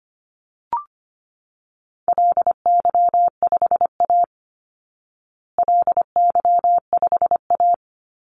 Morse code, heard as a keyed tone, about 25 wpm, 700 Hz. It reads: LY5A LY5A
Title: Call Signs Repeated 2x - 25wpm